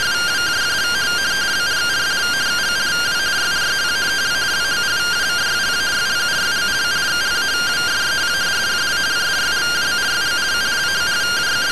Two-tone teleprinter signal, about 170 Hz shift.
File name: findings / MR AXIAL fr1 120.gen-04 rtty rx